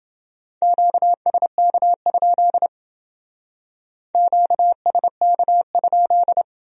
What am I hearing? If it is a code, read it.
Morse code, heard as a keyed tone, about 28 wpm, 700 Hz. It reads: QSK? QSK?